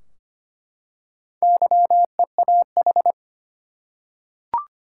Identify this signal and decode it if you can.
Morse code, heard as a keyed tone, about 25 wpm, 700 Hz. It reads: YEAH
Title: Four Letter Words - 25wpm